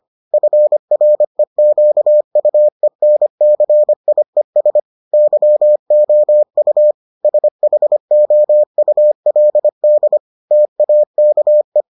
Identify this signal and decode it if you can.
Morse code, heard as a keyed tone, about 25 wpm, 600 Hz. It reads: FREQUENCIES YOU SHOULD TAKE